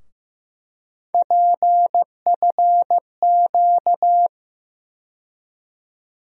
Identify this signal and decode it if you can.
Morse code, heard as a keyed tone, about 15 wpm, 700 Hz. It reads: PFQ
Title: USA Call Sign Postfixes - 3 Letters 15wpm